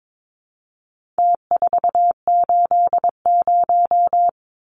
Morse code, heard as a keyed tone, about 22 wpm, 700 Hz. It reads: T480
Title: Numbers - Contest Style - 4-Digit 22wpm